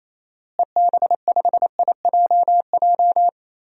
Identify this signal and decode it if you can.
Morse code, heard as a keyed tone, about 28 wpm, 700 Hz. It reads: EB5IJJ
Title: DXCC-100 Call Signs - 6 Characters - Repeated 2x - 28wpm